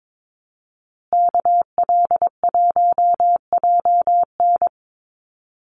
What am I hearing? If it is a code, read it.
Morse code, heard as a keyed tone, about 22 wpm, 700 Hz. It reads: KL1JN
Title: USA Call Signs Repeated 3x 22wpm